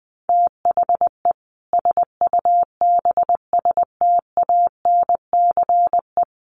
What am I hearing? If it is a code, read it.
Morse code, heard as a keyed tone, about 20 wpm, 700 Hz. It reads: THE SUBSTANCE
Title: Sentences from Top 1500 Words 20wpm